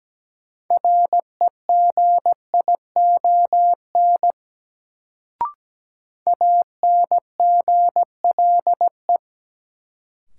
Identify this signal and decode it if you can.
Morse code, heard as a keyed tone, about 17 wpm, 700 Hz. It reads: REGION ANGLE
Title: Sentences from Top 1500 Words - New Word Drill Alt - 17wpm